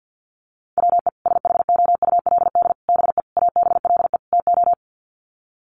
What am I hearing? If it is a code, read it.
Morse code, heard as a keyed tone, about 50 wpm, 700 Hz. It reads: WE SHOULD BE ABLE TO
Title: Sets of 5 Words Encore2 50wpm